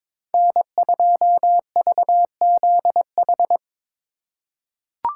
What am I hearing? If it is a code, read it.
Morse code, heard as a keyed tone, about 23 wpm, 700 Hz. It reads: N2VZH
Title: USA Call Signs - 5 Characters - 3x Character Spacing - 22wpm